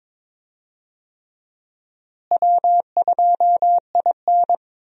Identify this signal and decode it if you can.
Morse code, heard as a keyed tone, about 21 wpm, 700 Hz. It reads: W2IN
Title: USA Call Signs - 4 Characters - Repeated 3x - 22wpm